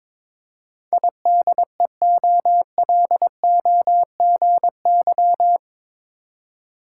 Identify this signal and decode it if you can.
Morse code, heard as a keyed tone, about 22 wpm, 700 Hz. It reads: IDEOLOGY
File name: practice / 22wpm / Eight Letter Words - 22wpm